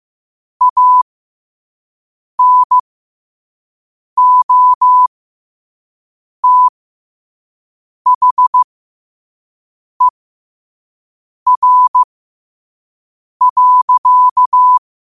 Morse code, heard as a keyed tone, about 15 wpm, spaced as for 5 wpm, 1000 Hz. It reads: ANOTHER.
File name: qotdfiles / 5wpm